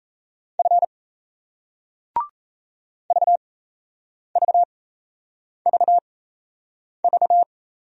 Morse code, heard as a keyed tone, about 38 wpm, spaced as for 8 wpm, 700 Hz. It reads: F VVVV